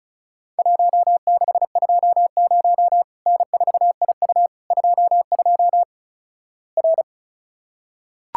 Morse code, heard as a keyed tone, about 36 wpm, 700 Hz. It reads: 1620 N4IU 22 R
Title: Worked All Europe DX Contest - QTC Exchange - 35wpm